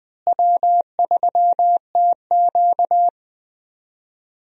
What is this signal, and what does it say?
Morse code, heard as a keyed tone, about 20 wpm, 700 Hz. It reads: W3TQ
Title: USA Call Signs - 4 Characters - Repeated 2x - 20wpm